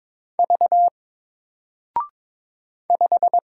Morse code, heard as a keyed tone, about 22 wpm, 700 Hz. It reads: V 5
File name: Single-Letter-Number-22wpm